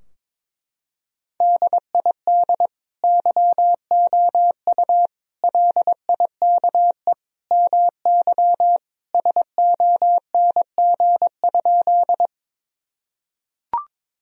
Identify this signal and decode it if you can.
Morse code, heard as a keyed tone, about 22 wpm, 700 Hz. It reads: DID YOU LIKE MY SONG?
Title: Sentences from Top 1000 Words - Review Part 2 - 22wpm